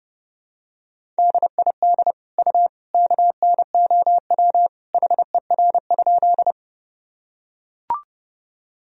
Morse code, heard as a keyed tone, about 30 wpm, 700 Hz. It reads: DID U KNOW HER?
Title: Sentences from Top 100 Words and Abbreviations 30wpm